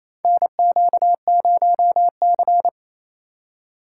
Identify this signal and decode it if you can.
Morse code, heard as a keyed tone, about 28 wpm, 700 Hz. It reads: NQ0C
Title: USA Call Signs - 4 Characters - 4x Character Spacing - 28wpm